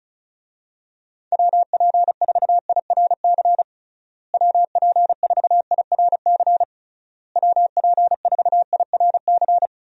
Morse code, heard as a keyed tone, about 36 wpm, 700 Hz. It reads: WP4IRC WP4IRC WP4IRC
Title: USA Call Signs - 6 Characters - Repeated 3x - 35wpm